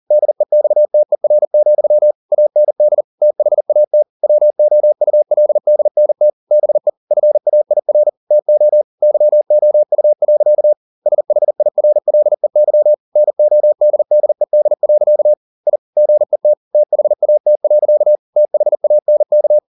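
Morse code, heard as a keyed tone, about 40 wpm, 600 Hz. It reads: DEXTER, AND THAT WOULDNT BE FAIR TO YOU. SHIRLEY NODDED. I GET THAT. THANK
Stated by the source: Morse code at 40 wpm, practice text, CW med Gnister.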